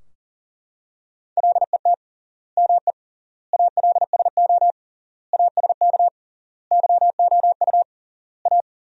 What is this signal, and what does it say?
Morse code, heard as a keyed tone, about 40 wpm, 700 Hz. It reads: LET ME ALSO ASK YOU A
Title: 3x Word-Spacing - Sentences from Top 400 Words - 40wpm